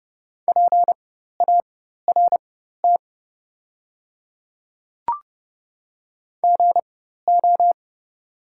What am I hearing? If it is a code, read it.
Morse code, heard as a keyed tone, about 31 wpm, spaced as for 14 wpm, 690 Hz. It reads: PART GO
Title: 4x Character Spacing - Top 500 Words - 30wpm